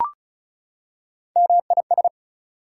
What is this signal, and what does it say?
Morse code, heard as a keyed tone, about 34 wpm, 700 Hz. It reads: MIS
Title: Prefixes and Suffixes 35wpm